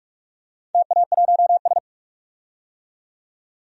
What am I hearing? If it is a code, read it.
Morse code, heard as a keyed tone, about 46 wpm, 700 Hz. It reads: TA1S
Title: DXCC-100 Call Signs - 4 Characters - 45wpm